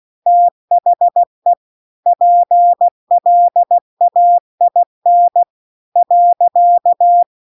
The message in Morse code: THE PLAIN .

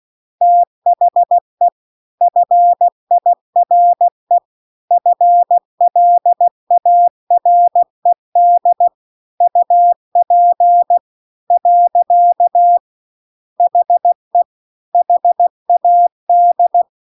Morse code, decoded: THE FIRE FLARED UP . HE HAD

